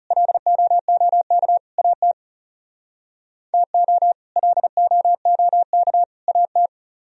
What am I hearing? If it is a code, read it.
Morse code, heard as a keyed tone, about 37 wpm, 700 Hz. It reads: LOOK AT TO LOOK AT
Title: Speed-Racing - Sets of 3 Words 30wpm